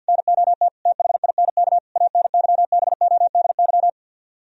Morse code, heard as a keyed tone, about 64 wpm, 700 Hz. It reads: NOT THINK ANYBODY